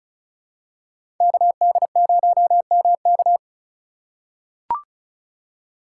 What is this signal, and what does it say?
Morse code, heard as a keyed tone, about 36 wpm, 700 Hz. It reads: KD0MK
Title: USA Call Signs - 2x Character Spacing - 35wpm